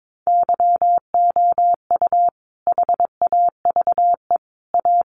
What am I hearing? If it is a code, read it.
Morse code, heard as a keyed tone, about 22 wpm, 700 Hz. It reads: YOU HAVE A
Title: Sets of 5 Words Encore 22wpm